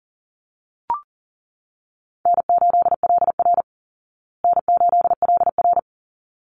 Morse code, heard as a keyed tone, about 40 wpm, 700 Hz. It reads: N8LR N8LR
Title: Easy Call Signs Repeated 2x - 40wpm